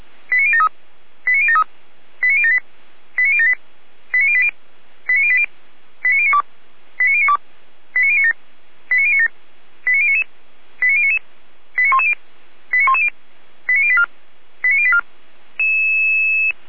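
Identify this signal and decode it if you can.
Five-tone selective call (ZVEI), heard as quick runs of five tones, tones of about 70 ms. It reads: 89073 89073 89078 89078 89080 89080 89092 89092 89097 89097 89099 89099 89110 89110 89084 89084
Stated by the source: ZVEI five-tone selective call received by SDR, from the Feuerwehr Orlamünde fire brigade (f85360) audio server.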